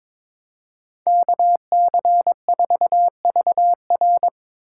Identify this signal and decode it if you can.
Morse code, heard as a keyed tone, about 21 wpm, 700 Hz. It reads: KC4VR
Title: USA Call Signs - 5 Characters - 3x Character Spacing - 22wpm